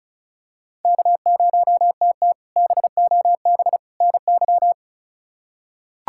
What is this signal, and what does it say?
Morse code, heard as a keyed tone, about 36 wpm, 700 Hz. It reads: K0TT BOB NY
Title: K1USN SST QSO Practice - 35wpm